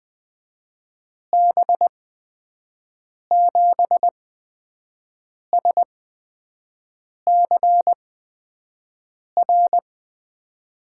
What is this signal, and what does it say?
Morse code, heard as a keyed tone, about 21 wpm, spaced as for 5 wpm, 700 Hz. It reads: B7SCR
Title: USA Call Signs - 8x Character Spacing - 20wpm